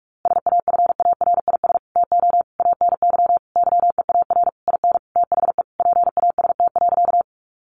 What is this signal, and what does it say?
Morse code, heard as a keyed tone, about 44 wpm, 700 Hz. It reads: SAFARIS TO ANY YEAR IN THE PAST.
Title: A Sound Of Thunder 45wpm